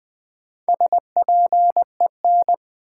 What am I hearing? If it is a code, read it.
Morse code, heard as a keyed tone, about 20 wpm, 700 Hz. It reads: SPEN
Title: Build A Word - 1 Letter At A Time - 5-Letter Words - 20wpm